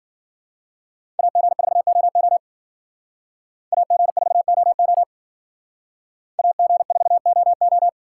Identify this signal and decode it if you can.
Morse code, heard as a keyed tone, about 53 wpm, 700 Hz. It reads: AG4OO AG4OO AG4OO